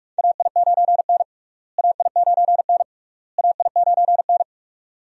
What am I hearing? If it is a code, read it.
Morse code, heard as a keyed tone, about 46 wpm, 700 Hz. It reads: AI9N AI9N AI9N
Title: USA Call Signs - 4 Characters - Repeated 3x - 45wpm